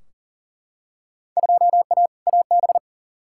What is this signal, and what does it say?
Morse code, heard as a keyed tone, about 40 wpm, 700 Hz. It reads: 2A AB
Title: Field Day Exchanges - 40wpm